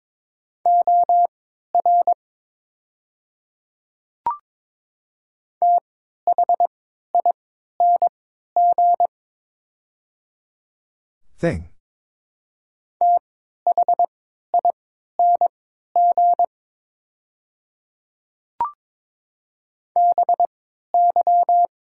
Morse code, heard as a keyed tone, about 23 wpm, spaced as for 13 wpm, 700 Hz. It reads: OR THING THING BY
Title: 3x Character Spacing - Top 100 Words - 22wpm